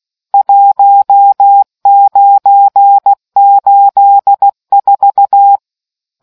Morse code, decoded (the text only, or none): 1984